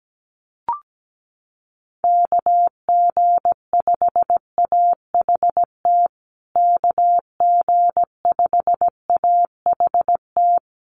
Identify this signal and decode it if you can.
Morse code, heard as a keyed tone, about 17 wpm, 700 Hz. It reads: KG5AHT KG5AHT
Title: USA Call Signs Repeated 3x 17wpm